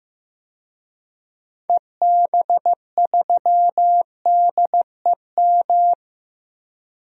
Morse code, heard as a keyed tone, about 15 wpm, 700 Hz. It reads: EB3DEM
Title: DXCC-100 Call Signs - 6 Characters - 15wpm